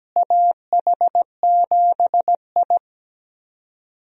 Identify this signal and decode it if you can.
Morse code, heard as a keyed tone, about 17 wpm, 700 Hz. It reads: AH7I
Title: USA Call Signs - 4 Characters - Repeated 3x - 17wpm